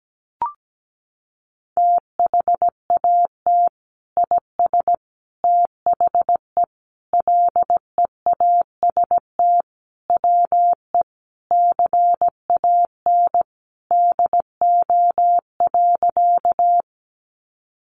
Morse code, heard as a keyed tone, about 17 wpm, 700 Hz. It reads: THAT IS THE LEAST WE CAN DO.
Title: Sentences from Top 700 Words 17wpm